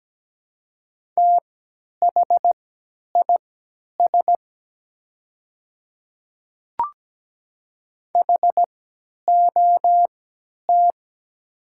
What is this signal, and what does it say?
Morse code, heard as a keyed tone, about 17 wpm, spaced as for 10 wpm, 700 Hz. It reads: THIS HOT
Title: 3x Character Spacing - Top 100 Words - 17wpm